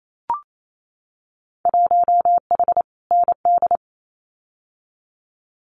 Morse code, heard as a keyed tone, about 28 wpm, 700 Hz. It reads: 1H ND
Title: Winter Field Day Exchanges 28wpm